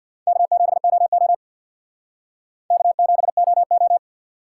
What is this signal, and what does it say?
Morse code, heard as a keyed tone, about 64 wpm, 700 Hz. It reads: K7OO K7OO